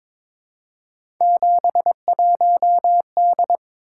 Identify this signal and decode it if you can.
Morse code, heard as a keyed tone, about 22 wpm, 700 Hz. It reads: 71D
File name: practice / 22wpm / Three Letter-Number Combinations - 22wpm